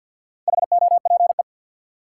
Morse code, heard as a keyed tone, about 52 wpm, 700 Hz. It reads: HOPE
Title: Sentences from Top 1000 Words - New Word Drill - 50wpm